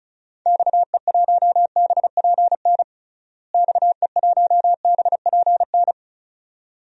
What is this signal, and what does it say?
Morse code, heard as a keyed tone, about 34 wpm, 700 Hz. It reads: XE1BPN XE1BPN
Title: DXCC-100 Call Signs - 6 Characters - Repeated 2x - 35wpm